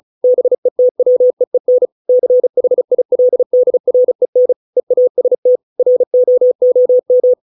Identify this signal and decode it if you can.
Morse code, heard as a keyed tone, about 34 wpm, 500 Hz. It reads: BETWEEN CHILDREN EAST ROOM